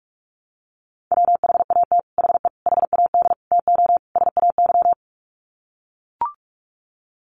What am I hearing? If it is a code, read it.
Morse code, heard as a keyed tone, about 44 wpm, 700 Hz. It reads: WHAT HE HAD TO SAY
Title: Sets of 5 Words Encore2 45wpm